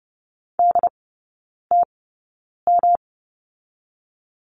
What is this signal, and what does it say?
Morse code, heard as a keyed tone, about 30 wpm, spaced as for 9 wpm, 700 Hz. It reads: DTM
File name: Lesson 18e - Letter m - Three Characters 30wpm